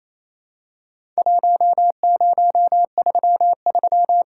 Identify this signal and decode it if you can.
Morse code, heard as a keyed tone, about 27 wpm, 700 Hz. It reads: 1033
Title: Worked All Europe DX Contest - QTC Exchange - 28wpm